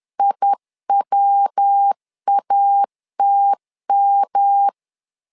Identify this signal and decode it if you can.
Morse code, heard as a keyed tone, about 10 wpm, 800 Hz. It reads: IWATM
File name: LEZIONI / 3 MWRO / 3 MW 10wpm 1959